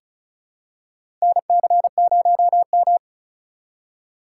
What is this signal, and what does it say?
Morse code, heard as a keyed tone, about 36 wpm, 700 Hz. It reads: NC0M
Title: USA Call Signs - 4 Characters - 2x Character Spacing - 35wpm